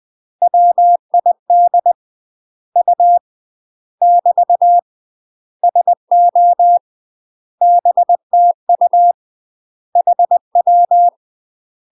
Morse code, heard as a keyed tone, about 20 wpm, 700 Hz. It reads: WID U <BT> SO BTU HW
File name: UK MASTER QSOs TRAINING / 20WPM / UK MASTER QSOs 0039